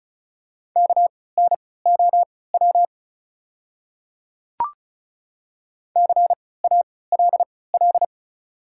Morse code, heard as a keyed tone, about 36 wpm, spaced as for 20 wpm, 690 Hz. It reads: KNOW CALL
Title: 3x Character Spacing - Top 100 Words - 35wpm